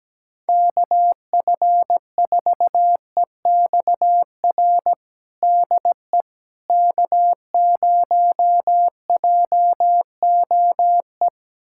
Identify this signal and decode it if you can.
Morse code, heard as a keyed tone, about 17 wpm, 700 Hz. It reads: KF4EXR DE K0JOE
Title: USA Callsign DE Callsign 17wpm